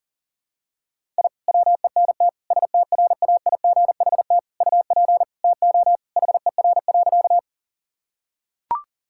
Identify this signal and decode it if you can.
Morse code, heard as a keyed tone, about 40 wpm, 690 Hz. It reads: I WENT STRAIGHT UP TO HER.
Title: Sentences from Top 1000 Words - Review Part 6 - 40wpm